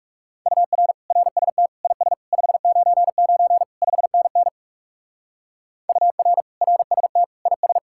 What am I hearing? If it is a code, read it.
Morse code, heard as a keyed tone, about 44 wpm, 700 Hz. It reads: UR RST IS 599 5NN UR RST IS